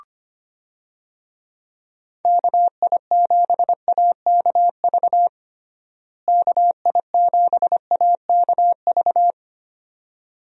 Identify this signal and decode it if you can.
Morse code, heard as a keyed tone, about 25 wpm, 700 Hz. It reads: KI7AKV KI7AKV
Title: USA Call Signs - 6 Characters - Repeated 2x - 25wpm